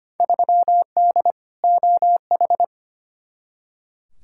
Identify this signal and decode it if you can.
Morse code, heard as a keyed tone, about 25 wpm, 700 Hz. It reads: 3D OH
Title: Field Day Exchanges - 25wpm